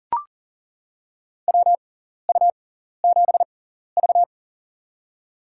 Morse code, heard as a keyed tone, about 40 wpm, spaced as for 14 wpm, 700 Hz. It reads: WU7V
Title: USA Call Signs - 6x Character Spacing - 40wpm